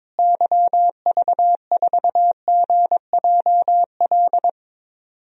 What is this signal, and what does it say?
Morse code, heard as a keyed tone, about 21 wpm, 700 Hz. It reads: YV4GJL
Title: DXCC-100 Call Signs - 6 Characters - Repeated 2x - 22wpm